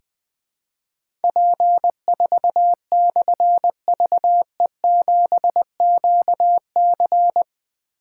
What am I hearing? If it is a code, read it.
Morse code, heard as a keyed tone, about 20 wpm, 700 Hz. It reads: P4/VE7QC
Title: Difficult Call Signs 20wpm